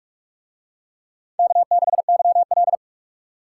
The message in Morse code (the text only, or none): K6YL